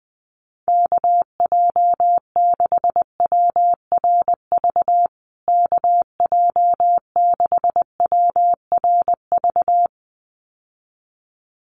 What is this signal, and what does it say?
Morse code, heard as a keyed tone, about 20 wpm, 700 Hz. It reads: KJ6WRV KJ6WRV
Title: USA Call Signs Repeated 2x 20wpm